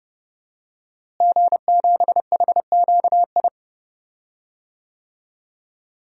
Morse code, heard as a keyed tone, about 31 wpm, 700 Hz. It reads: G7HQI
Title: DXCC-100 Call Signs - 5 Characters - Repeated 2x - 30wpm